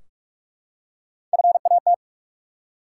Morse code, heard as a keyed tone, about 44 wpm, 700 Hz. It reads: FAT
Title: Three Letter Words - 45wpm